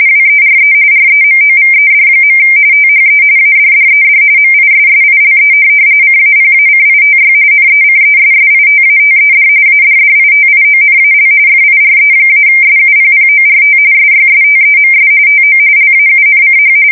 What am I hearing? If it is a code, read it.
RTTY, about 45.45 baud. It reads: WERE REMOVED. THE ADDED MISSILE CAPACITY NECESSITATED ADDITIONAL FIRE-SUPPORT SYSTEMS TO LAUNCH
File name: fsk170 / USS-Missouri